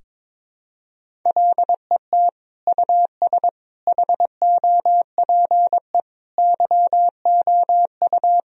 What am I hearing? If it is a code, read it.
Morse code, heard as a keyed tone, about 22 wpm, 700 Hz. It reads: LET US HOPE YOU
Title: Sentences from Top 1000 Words - Review Part 1 - 22wpm